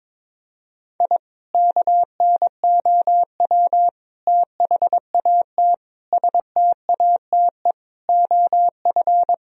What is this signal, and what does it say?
Morse code, heard as a keyed tone, about 22 wpm, 700 Hz. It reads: I KNOW THAT STATE OF
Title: Sentences from Top 1000 Words - Review Part 2 - 22wpm